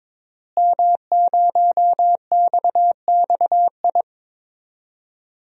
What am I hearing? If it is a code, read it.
Morse code, heard as a keyed tone, about 22 wpm, 700 Hz. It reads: M0XXI
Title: DXCC-100 Call Signs - 5 Characters - Repeated 3x - 22wpm